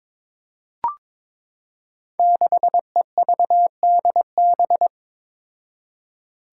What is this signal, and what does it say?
Morse code, heard as a keyed tone, about 22 wpm, 700 Hz. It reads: 6EVDB
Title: Five Challenging Letter-Number Combinations 22wpm